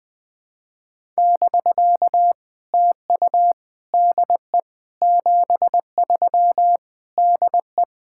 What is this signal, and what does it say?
Morse code, heard as a keyed tone, about 21 wpm, 700 Hz. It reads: <BK> TU DE 73 DE
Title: POTA QSO Practice - 20wpm